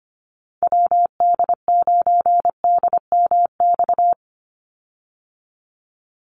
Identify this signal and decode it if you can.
Morse code, heard as a keyed tone, about 25 wpm, 700 Hz. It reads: WD9DMX
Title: USA Call Signs 25wpm